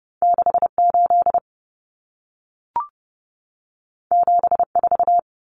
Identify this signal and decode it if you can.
Morse code, heard as a keyed tone, about 30 wpm, 700 Hz. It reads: 68 74
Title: Numbers - 2-Digit 30wpm